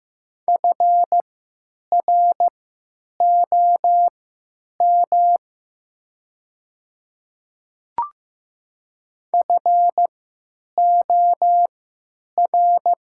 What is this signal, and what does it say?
Morse code, heard as a keyed tone, about 15 wpm, spaced as for 9 wpm, 700 Hz. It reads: FROM FOR
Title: 3x Character Spacing - Top 100 Words - 15wpm